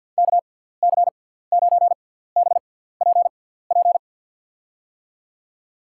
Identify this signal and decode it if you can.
Morse code, heard as a keyed tone, about 51 wpm, spaced as for 17 wpm, 700 Hz. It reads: KC9BPP
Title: USA Call Signs - 6 Characters - 6x Character Spacing - 50wpm